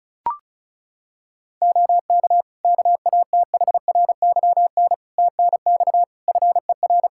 Morse code, heard as a keyed tone, about 36 wpm, 700 Hz. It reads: OK KATHRYN TNX FER
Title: Practice QSO - Exchange 4 - Ending - 35wpm